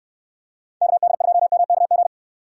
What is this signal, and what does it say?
Morse code, heard as a keyed tone, about 69 wpm, 700 Hz. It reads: KD2MKG